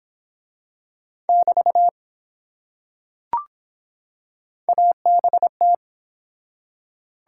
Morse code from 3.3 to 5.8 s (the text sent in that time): ABT